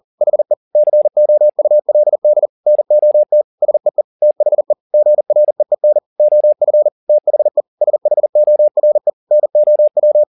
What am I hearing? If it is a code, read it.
Morse code, heard as a keyed tone, about 40 wpm, 600 Hz. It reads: HE COULD NOT SEE THE GREEN OF THE SHORE NOW